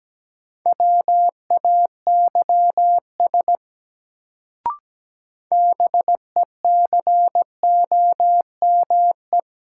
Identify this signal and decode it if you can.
Morse code, heard as a keyed tone, about 17 wpm, 700 Hz. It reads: WAYS BECOME
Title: Sets of 5 Words Encore - New Word Drill - 17wpm